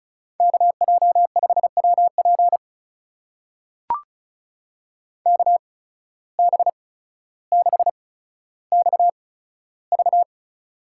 Morse code, heard as a keyed tone, about 36 wpm, spaced as for 15 wpm, 700 Hz. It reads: KJ5WP KB6XV
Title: USA Call Signs - 5 Characters - 8x Character Spacing - 35wpm